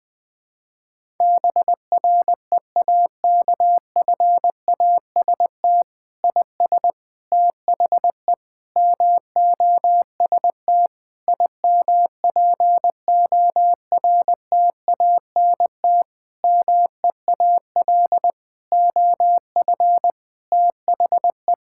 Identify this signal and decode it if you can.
Morse code, heard as a keyed tone, about 20 wpm, 700 Hz. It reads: BREAKFAST IS THE MOST IMPORTANT MEAL OF THE
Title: Sentences with 9-Word Phrases - 20wpm